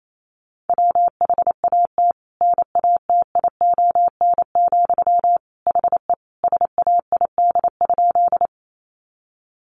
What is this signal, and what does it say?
Morse code, heard as a keyed tone, about 28 wpm, 700 Hz. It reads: WHAT NATION, HE SAID?
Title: Sentences from Top 900 Words 28wpm